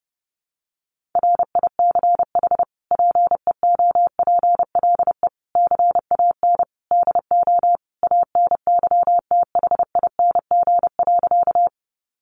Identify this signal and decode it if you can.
Morse code, heard as a keyed tone, about 30 wpm, 700 Hz. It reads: RICH PEOPLE CAN DO ANYTHING.
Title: Sentences from Top 1100 Words 30wpm